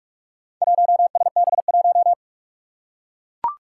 Morse code, heard as a keyed tone, about 42 wpm, 700 Hz. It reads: 1SB1
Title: Four Challenging Letter-Number Combinations 45wpm